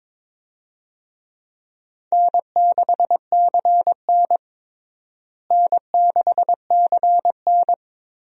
Morse code, heard as a keyed tone, about 21 wpm, 700 Hz. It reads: N6CN N6CN
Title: USA Call Signs - 4 Characters - Repeated 3x - 22wpm